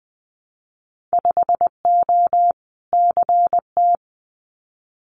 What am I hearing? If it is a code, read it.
Morse code, heard as a keyed tone, about 20 wpm, 700 Hz. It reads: 5O CT
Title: Winter Field Day Exchanges Encore 20wpm